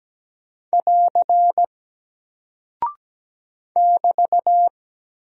Morse code, heard as a keyed tone, about 17 wpm, 700 Hz. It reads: <AR> <BT>
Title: Single Character (Letter:Number:Punctuation:Prosign) 17wpm